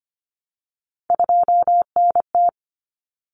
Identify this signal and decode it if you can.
Morse code, heard as a keyed tone, about 25 wpm, 700 Hz. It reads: 2NT
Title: Common Cut Numbers - 3-Digit 25wpm